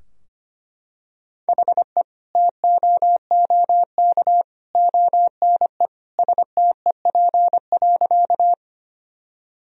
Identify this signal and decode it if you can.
Morse code, heard as a keyed tone, about 25 wpm, 700 Hz. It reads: HE TOOK ONE STEP.
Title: Sentences from Top 1000 Words - Review Part 2 - 25wpm